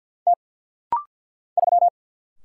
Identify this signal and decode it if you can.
Morse code, heard as a keyed tone, about 49 wpm, 700 Hz. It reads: T 3
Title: Single Letter-Number Alt - Rapid-Fire 50wpm